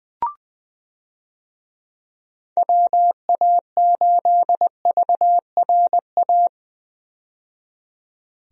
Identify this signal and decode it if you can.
Morse code, heard as a keyed tone, about 20 wpm, 700 Hz. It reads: WA8VRA
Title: USA Call Signs - 6 Characters - 20wpm